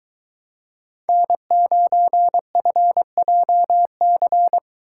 Morse code, heard as a keyed tone, about 22 wpm, 700 Hz. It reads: N9FJC